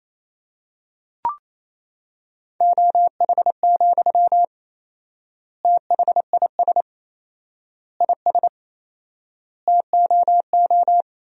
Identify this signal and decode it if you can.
Morse code, heard as a keyed tone, about 28 wpm, 700 Hz. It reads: OH, THIS IS TOO
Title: 4x Word-Spacing - Sentences from Top 500 Words - 28wpm